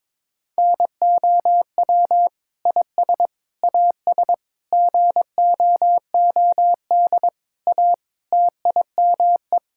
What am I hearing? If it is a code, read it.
Morse code, heard as a keyed tone, about 22 wpm, 700 Hz. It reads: NOW IS AS GOOD A TIME
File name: practice / 22wpm / Sentences with 8-Word Phrases - 22wpm